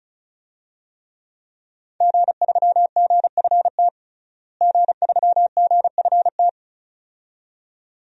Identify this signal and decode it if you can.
Morse code, heard as a keyed tone, about 35 wpm, 700 Hz. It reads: G3GFT G3GFT
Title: DXCC-100 Call Signs - 5 Characters - Repeated 2x - 35wpm